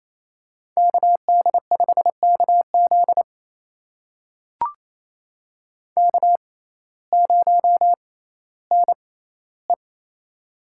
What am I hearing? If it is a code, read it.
Morse code, heard as a keyed tone, about 28 wpm, spaced as for 15 wpm, 700 Hz. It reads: KD5KZ K0NE
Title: USA Call Signs - 6x Character Spacing - 28wpm